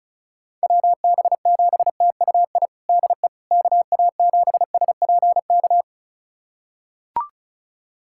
Morse code, heard as a keyed tone, about 36 wpm, 700 Hz. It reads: WB7TUI DE KA7SPK
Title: USA Callsign DE Callsign 35wpm